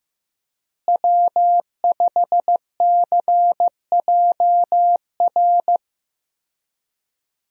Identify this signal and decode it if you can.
Morse code, heard as a keyed tone, about 15 wpm, 700 Hz. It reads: W5CJR
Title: USA Call Signs - 5 Characters - Repeated 3x - 15wpm